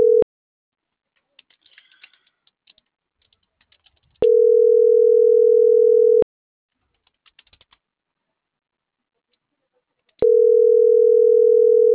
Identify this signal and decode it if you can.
Ringback tone (North America), 440 + 480 Hz.
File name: CENTRAL MAERSK / out gabrielle.melendez - .48129